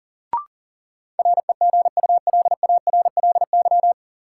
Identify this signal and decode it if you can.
Morse code, heard as a keyed tone, about 40 wpm, 700 Hz. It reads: REGULARLY
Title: Sentences from Top 1400 Words - New Word Drill - 40wpm